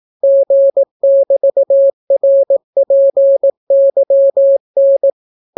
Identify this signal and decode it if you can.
Morse code, heard as a keyed tone, about 18 wpm, 550 Hz. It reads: G<BT>RPYN